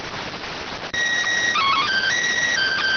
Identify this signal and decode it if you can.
SSTV (slow-scan television), PD 290.